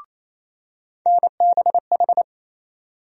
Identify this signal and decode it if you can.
Morse code, heard as a keyed tone, about 28 wpm, 700 Hz. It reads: NBH
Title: USA Call Sign Postfixes - 3 Letters 28wpm